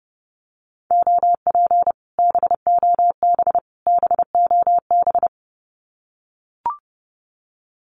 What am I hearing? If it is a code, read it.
Morse code, heard as a keyed tone, about 30 wpm, 700 Hz. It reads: OP BOB BOB
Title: Common QSO Phrases Encore 30wpm